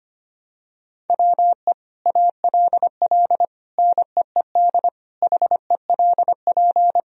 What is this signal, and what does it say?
Morse code, heard as a keyed tone, about 25 wpm, 700 Hz. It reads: WE ALL NEED HELP
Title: Sentences from Top 1000 Words - Review Part 3 - 25wpm